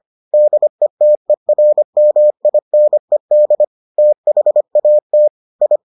Morse code, heard as a keyed tone, about 25 wpm, 600 Hz. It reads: DETERMINED THAT I